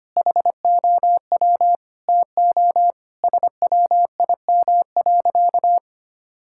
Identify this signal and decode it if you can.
Morse code, heard as a keyed tone, about 25 wpm, 700 Hz. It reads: HOW TO SWIM.
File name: practice / 25wpm / Sentences with 3-Word Phrases - 25wpm